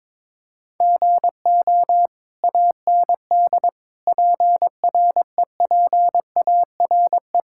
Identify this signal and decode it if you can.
Morse code, heard as a keyed tone, about 22 wpm, 700 Hz. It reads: GO AND PREPARE
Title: Sentences from Top 1000 Words - Review Part 6 - 22wpm